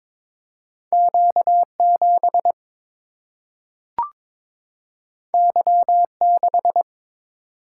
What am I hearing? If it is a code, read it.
Morse code, heard as a keyed tone, about 22 wpm, 700 Hz. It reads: Q7 Y6
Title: Single-Letter Single-Number 22wpm